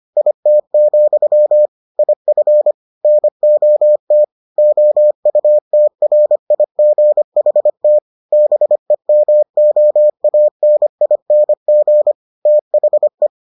Morse code, heard as a keyed tone, about 25 wpm, 600 Hz. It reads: IT, IF NOT OUTRIGHT BEMOANING THE